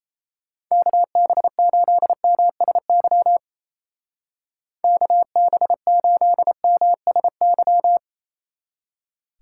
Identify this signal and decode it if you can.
Morse code, heard as a keyed tone, about 34 wpm, 700 Hz. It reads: KB8MSY KB8MSY
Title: Speed Racing - USA Call Signs 28wpm